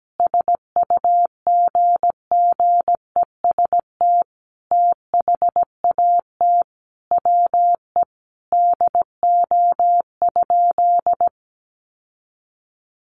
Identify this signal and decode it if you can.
Morse code, heard as a keyed tone, about 17 wpm, 700 Hz. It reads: SUGGEST THAT WE DO?
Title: Sentences from Top 1300 Words 17wpm